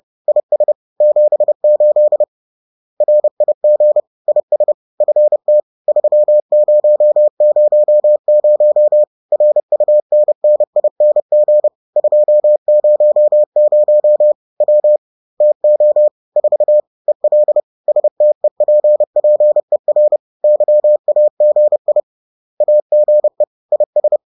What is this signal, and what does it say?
Morse code, heard as a keyed tone, about 30 wpm, 600 Hz. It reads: IS 78 RIG IS FT 3000 RUNNING 200 W TO 4 EL STEPPER YAGI AGE IS
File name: 30wpm-4